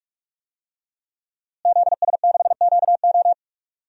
Morse code, heard as a keyed tone, about 43 wpm, 700 Hz. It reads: ZS6QO
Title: DXCC-100 Call Signs - 5 Characters - Repeated 3x - 45wpm